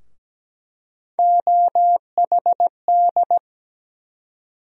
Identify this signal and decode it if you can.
Morse code, heard as a keyed tone, about 17 wpm, 700 Hz. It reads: OHD
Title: USA Call Sign Postfixes - 3 Letters 17wpm